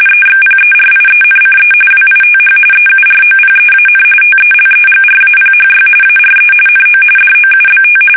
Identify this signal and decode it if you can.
RTTY, about 45.45 baud. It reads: EXPERIENCE WITH EUROPEAN TELEGRAPHY ARE NO DOUBT
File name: fsk850